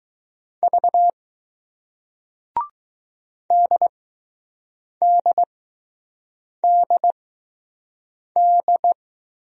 Morse code, heard as a keyed tone, about 20 wpm, 700 Hz. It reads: V D D D D